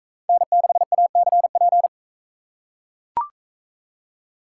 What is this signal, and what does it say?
Morse code, heard as a keyed tone, about 41 wpm, 700 Hz. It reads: N6ACP